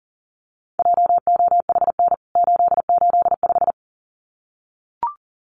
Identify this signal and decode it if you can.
Morse code, heard as a keyed tone, about 40 wpm, 700 Hz. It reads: JOHN 885
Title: CWT Exchange 40wpm